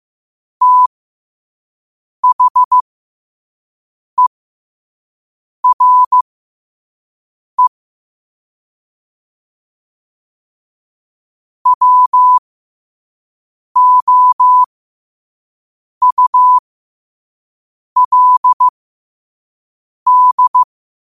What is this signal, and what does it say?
Morse code, heard as a keyed tone, about 15 wpm, spaced as for 5 wpm, 1000 Hz. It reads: THERE WOULD
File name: qotdfiles / 5wpm